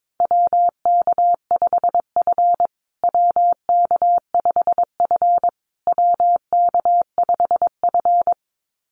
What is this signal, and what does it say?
Morse code, heard as a keyed tone, about 22 wpm, 690 Hz. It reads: WK5F WK5F WK5F
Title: USA Call Signs Repeated 3x 22wpm